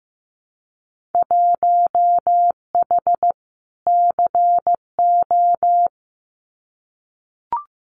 Morse code, heard as a keyed tone, about 15 wpm, 700 Hz. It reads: 1H CO
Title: Winter Field Day Exchanges 15wpm